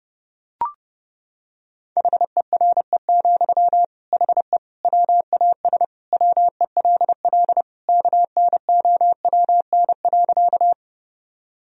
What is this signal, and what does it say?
Morse code, heard as a keyed tone, about 30 wpm, 700 Hz. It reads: HERE, HE WAS WELL KNOWN.
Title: Sentences from Top 1000 Words - Review Part 3 - 30wpm